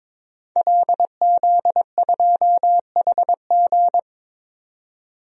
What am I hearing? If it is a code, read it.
Morse code, heard as a keyed tone, about 22 wpm, 700 Hz. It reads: LZ2HG
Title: DXCC-100 Call Signs - 5 Characters - 22wpm